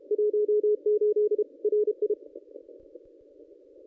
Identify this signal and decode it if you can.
Morse code, heard as a keyed tone, about 31 wpm, 440 Hz. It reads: 18 RI
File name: contests / ss-cw-k5tr-op-n5rz -1200